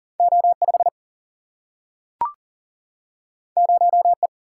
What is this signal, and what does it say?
Morse code, heard as a keyed tone, about 41 wpm, 700 Hz. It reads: O5 0E
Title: Single-Letter Single-Number 40wpm